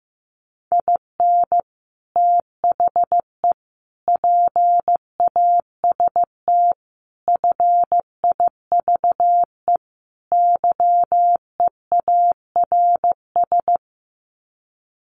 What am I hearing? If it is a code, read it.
Morse code, heard as a keyed tone, about 15 wpm, 700 Hz. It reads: IN THE PAST FIVE YEARS
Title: Sets of 5 Words Encore 15wpm